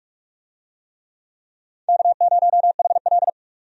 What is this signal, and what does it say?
Morse code, heard as a keyed tone, about 45 wpm, 700 Hz. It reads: K0HL
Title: USA Call Signs - 4 Characters - 45wpm